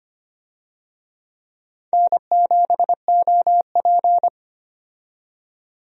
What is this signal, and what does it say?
Morse code, heard as a keyed tone, about 25 wpm, 700 Hz. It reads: N7OP
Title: USA Call Signs - 4 Characters - 25wpm